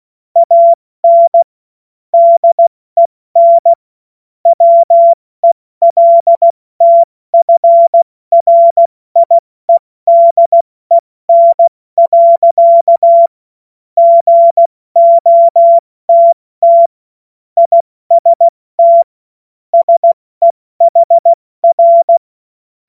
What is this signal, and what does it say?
Morse code, heard as a keyed tone, about 16 wpm, 680 Hz. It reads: AN DEN WELTFRIEDEN. GOTT IST SEHR
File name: Morsekurs / Witze / Witz 09. Februar 2021, 70 F80